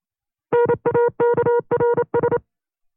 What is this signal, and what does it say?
Morse code, heard as a keyed tone, about 28 wpm, 480 Hz. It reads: NAKRS
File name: FreeFiles / CW-practice / 25WPM-side2